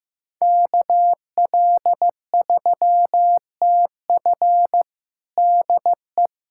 Morse code, heard as a keyed tone, about 15 wpm, 700 Hz. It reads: KL3TF DE
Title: USA Callsign DE Callsign - 5 Characters - 15wpm